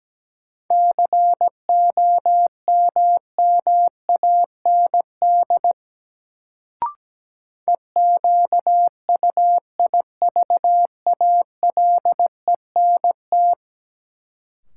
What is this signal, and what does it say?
Morse code, heard as a keyed tone, about 17 wpm, 700 Hz. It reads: COMMAND EQUIVALENT
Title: Sentences from Top 1600 Words - New Word Drill - 17wpm